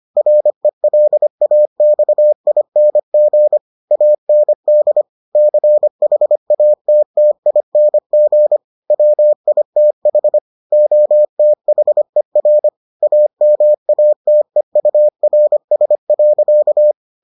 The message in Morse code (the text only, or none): RELAXING AND CHATTING WITH OTHER AMATEURS.